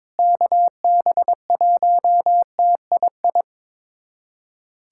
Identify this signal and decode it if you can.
Morse code, heard as a keyed tone, about 22 wpm, 700 Hz. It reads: KB1TII
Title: USA Callsign DE Callsign 22wpm